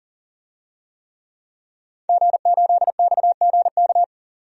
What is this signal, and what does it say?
Morse code, heard as a keyed tone, about 40 wpm, 700 Hz. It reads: G8XGK
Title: DXCC-100 Call Signs - 5 Characters - 40wpm